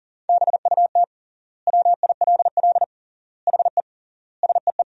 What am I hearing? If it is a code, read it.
Morse code, heard as a keyed tone, about 40 wpm, 700 Hz. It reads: BUT WILL HE SEE
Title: 3x Word-Spacing - Sentences from Top 100 Words - 40wpm